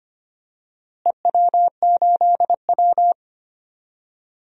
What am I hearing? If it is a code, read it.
Morse code, heard as a keyed tone, about 25 wpm, 700 Hz. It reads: EW8W
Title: DXCC-100 Call Signs - 4 Characters - Repeated 3x - 25wpm